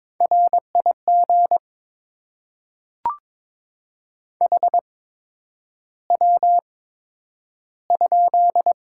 Morse code, heard as a keyed tone, about 22 wpm, spaced as for 9 wpm, 700 Hz. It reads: RIG HW?
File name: Top 20 QSO Elements - 8x Character Spacing - 22wpm